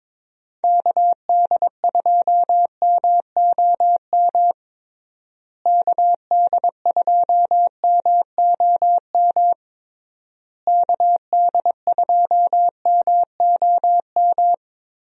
Morse code, heard as a keyed tone, about 22 wpm, 700 Hz. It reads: KD2MOM KD2MOM KD2MOM
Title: USA Call Signs - 6 Characters - Repeated 3x - 22wpm